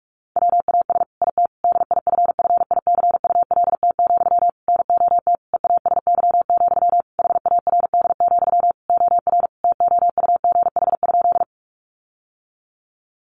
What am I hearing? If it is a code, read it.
Morse code, heard as a keyed tone, about 44 wpm, 700 Hz. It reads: WAS IT DIFFICULT, NOT EASY, HARD, OR TOUGH?
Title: Sentences from Top 1200 Words 45wpm